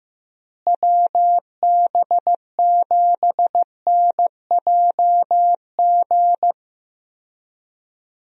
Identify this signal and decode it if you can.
Morse code, heard as a keyed tone, about 15 wpm, 700 Hz. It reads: WB7NJG
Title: USA Call Signs - 6 Characters - Repeated 3x - 15wpm